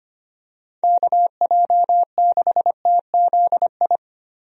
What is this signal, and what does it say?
Morse code, heard as a keyed tone, about 25 wpm, 700 Hz. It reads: KJ6TZI
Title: USA Call Signs - 6 Characters - Repeated 2x - 25wpm